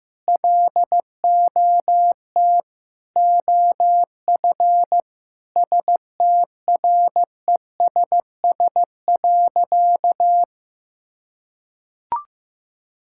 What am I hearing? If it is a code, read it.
Morse code, heard as a keyed tone, about 15 wpm, 700 Hz. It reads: LOT OF STRESS.
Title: Sentences from Top 1000 Words - Review Part 6 - 15wpm